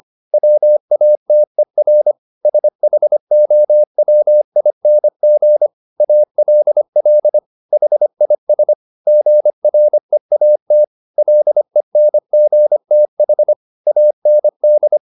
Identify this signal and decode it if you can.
Morse code, heard as a keyed tone, about 25 wpm, 600 Hz. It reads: WATER SHOWING ALL HIS GREAT LENGTH AND